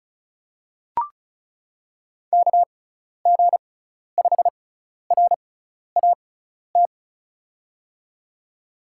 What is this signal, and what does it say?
Morse code, heard as a keyed tone, about 36 wpm, spaced as for 12 wpm, 700 Hz. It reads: KG5RAT
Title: USA Call Signs - 6 Characters - 6x Character Spacing - 35wpm